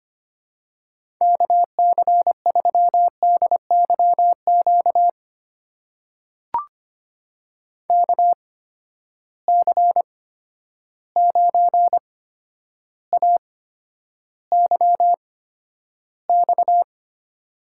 Morse code, heard as a keyed tone, about 25 wpm, spaced as for 11 wpm, 700 Hz. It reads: KC3DYQ KC9AYX
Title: USA Call Signs - 6 Characters - 8x Character Spacing - 25wpm